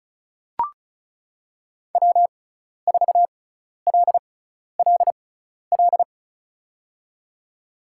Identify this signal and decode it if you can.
Morse code, heard as a keyed tone, about 34 wpm, spaced as for 12 wpm, 700 Hz. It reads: W4LLL
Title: USA Call Signs - 5 Characters - 6x Character Spacing - 35wpm